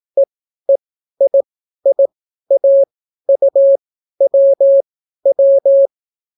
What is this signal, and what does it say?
Morse code, heard as a keyed tone, about 18 wpm, spaced as for 12 wpm, 550 Hz. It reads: EEIIAUWW